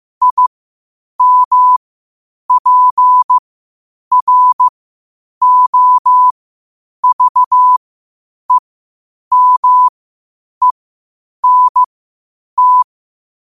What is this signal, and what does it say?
Morse code, heard as a keyed tone, about 15 wpm, spaced as for 8 wpm, 1000 Hz. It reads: IMPROVEMENT
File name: qotdfiles / 8wpm